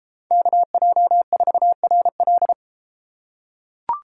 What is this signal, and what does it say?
Morse code, heard as a keyed tone, about 30 wpm, 700 Hz. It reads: KJ4RL